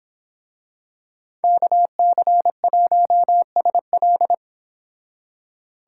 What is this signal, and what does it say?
Morse code, heard as a keyed tone, about 26 wpm, 700 Hz. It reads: KC1SL
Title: Speed Racing - USA Call Signs 22wpm